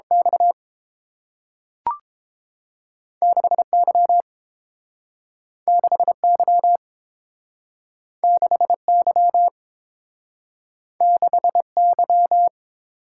Morse code, heard as a keyed tone, about 30 wpm, 700 Hz. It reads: X 6Y 6Y 6Y 6Y